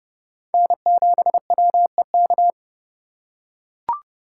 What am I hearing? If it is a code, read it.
Morse code, heard as a keyed tone, about 26 wpm, 700 Hz. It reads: N7WEK